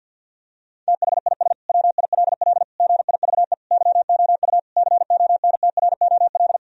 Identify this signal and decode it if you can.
Morse code, heard as a keyed tone, about 50 wpm, 700 Hz. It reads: THIS WILL GIVE YOU CONTROL
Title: Sentences from Top 1000 Words - Review Part 3 - 50wpm